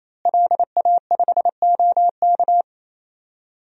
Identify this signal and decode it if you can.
Morse code, heard as a keyed tone, about 27 wpm, 700 Hz. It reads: LA5OK
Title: DXCC-100 Call Signs - 5 Characters - Repeated 3x - 28wpm